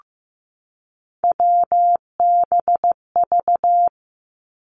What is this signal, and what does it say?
Morse code, heard as a keyed tone, about 15 wpm, 700 Hz. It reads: WBV
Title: Three Challenging Letter-Number Combinations 15wpm